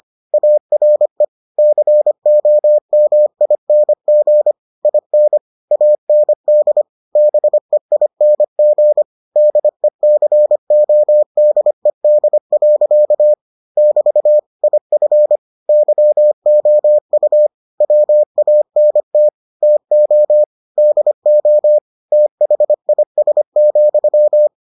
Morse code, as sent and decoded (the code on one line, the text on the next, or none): .- .-. / -.-. --- -- .. -. --. / .. -. / .- -. -.. / -... .. -. --. / -.. -.-. --- -.. -.. .-.-.- / -...- / .. ..-. / -.-- --- ..- / .-- .- -. - / - --- / -.. --- / - .... .. ... --..--
ARE COMING IN AND BEING DECODED. <BT> IF YOU WANT TO DO THIS,